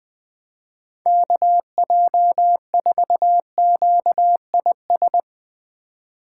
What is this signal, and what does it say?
Morse code, heard as a keyed tone, about 20 wpm, 700 Hz. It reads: KJ4QIS
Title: USA Call Signs - 6 Characters - 4x Character Spacing - 20wpm